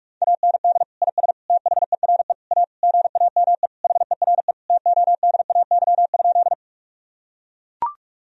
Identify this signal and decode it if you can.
Morse code, heard as a keyed tone, about 44 wpm, 700 Hz. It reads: AND IS THERE A GAME HERE TODAY?
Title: Sentences from Top 1000 Words - Review Part 3 - 45wpm